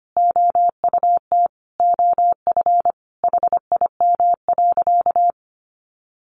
Morse code, heard as a keyed tone, about 25 wpm, 700 Hz. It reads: OUT OF HIM.
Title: Sentences from Top 1500 Words 25wpm